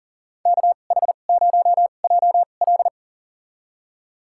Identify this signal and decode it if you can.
Morse code, heard as a keyed tone, about 40 wpm, spaced as for 29 wpm, 700 Hz. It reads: KH0JL
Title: USA Call Signs - 5 Characters - 2x Character Spacing - 40wpm